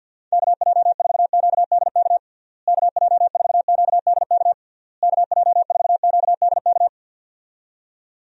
Morse code, heard as a keyed tone, about 52 wpm, 700 Hz. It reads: KJ4QDK KJ4QDK KJ4QDK
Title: USA Call Signs - 6 Characters - Repeated 3x - 50wpm